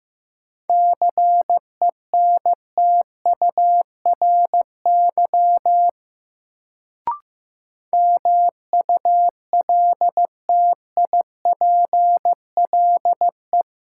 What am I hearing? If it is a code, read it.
Morse code, heard as a keyed tone, about 15 wpm, 700 Hz. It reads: CENTURY MULTIPLE
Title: Sets of 5 Words - New Word Drill - 15wpm